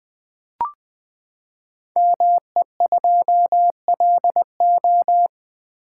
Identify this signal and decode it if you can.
Morse code, heard as a keyed tone, about 20 wpm, 700 Hz. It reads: ME2LO
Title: Code Groups - 5 Letter:Numbers - 20wpm